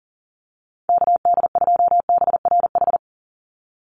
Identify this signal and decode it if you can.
Morse code, heard as a keyed tone, about 40 wpm, 700 Hz. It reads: KD2BRH
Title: USA Call Signs 40wpm